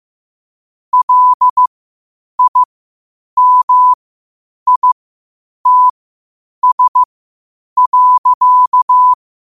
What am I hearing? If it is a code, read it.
Morse code, heard as a keyed tone, about 15 wpm, spaced as for 8 wpm, 1000 Hz. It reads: LIMITS.
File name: qotdfiles / 8wpm